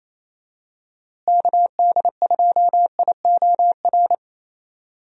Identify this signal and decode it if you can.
Morse code, heard as a keyed tone, about 28 wpm, 700 Hz. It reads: KD2IOR
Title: USA Call Signs - 6 Characters - Repeated 2x - 28wpm